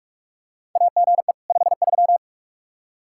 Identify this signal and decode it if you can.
Morse code, heard as a keyed tone, about 44 wpm, 700 Hz. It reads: AGE 53
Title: Practice QSO - Exchange 3 - More Info - 45wpm